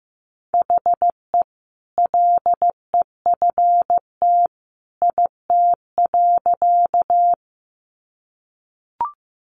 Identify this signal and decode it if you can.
Morse code, heard as a keyed tone, about 15 wpm, 700 Hz. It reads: HE LEFT IT.
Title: A Sound Of Thunder 15wpm